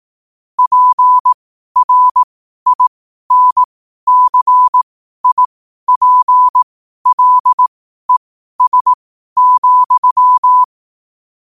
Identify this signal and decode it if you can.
Morse code, heard as a keyed tone, about 18 wpm, spaced as for 12 wpm, 1000 Hz. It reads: PRINCIPLES,
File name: qotdfiles / 12wpm